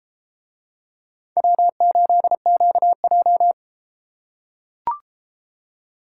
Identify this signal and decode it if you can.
Morse code, heard as a keyed tone, about 31 wpm, 700 Hz. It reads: W8QJ